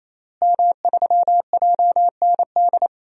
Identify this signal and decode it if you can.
Morse code, heard as a keyed tone, about 28 wpm, 700 Hz. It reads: M3JND
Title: DXCC-100 Call Signs - 5 Characters - Repeated 3x - 28wpm